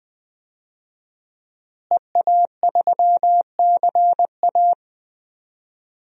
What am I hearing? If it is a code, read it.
Morse code, heard as a keyed tone, about 20 wpm, 700 Hz. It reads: EA3CA
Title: DXCC-100 Call Signs - 5 Characters - Repeated 3x - 20wpm